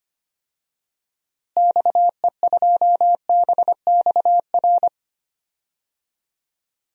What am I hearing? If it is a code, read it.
Morse code, heard as a keyed tone, about 25 wpm, 700 Hz. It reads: XE2BXR
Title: DXCC-100 Call Signs - 6 Characters - Repeated 3x - 25wpm